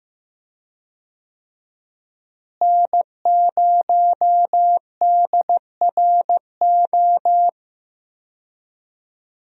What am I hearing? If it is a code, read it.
Morse code, heard as a keyed tone, about 15 wpm, 700 Hz. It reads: N0DRO
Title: USA Call Signs - 5 Characters - Repeated 3x - 15wpm